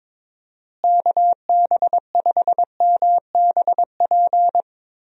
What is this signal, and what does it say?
Morse code, heard as a keyed tone, about 22 wpm, 700 Hz. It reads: KB5MBP
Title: USA Call Signs - 6 Characters - Repeated 2x - 22wpm